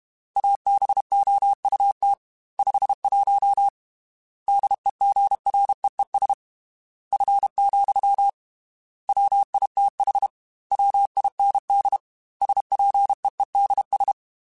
Morse code, heard as a keyed tone, about 32 wpm, 800 Hz. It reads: ABOUT 51 DEGREES F, WITH WIND SPEEDS
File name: Weather Sets / Weather Sets-mp3s / Weather 4 21 @30